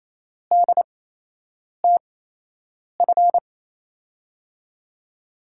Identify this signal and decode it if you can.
Morse code, heard as a keyed tone, about 27 wpm, spaced as for 8 wpm, 700 Hz. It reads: DTF
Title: USA Call Signs - 5 Characters - 8x Character Spacing - 28wpm